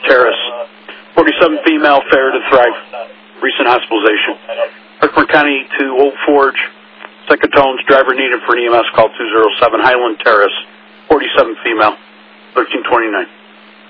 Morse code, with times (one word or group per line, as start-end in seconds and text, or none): none